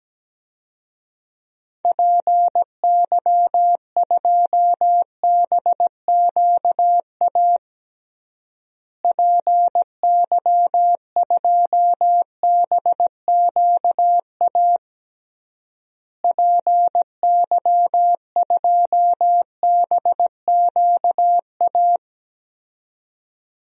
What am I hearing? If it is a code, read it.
Morse code, heard as a keyed tone, about 17 wpm, 700 Hz. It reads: PY2BQA PY2BQA PY2BQA
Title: DXCC-100 Call Signs - 6 Characters - Repeated 3x - 17wpm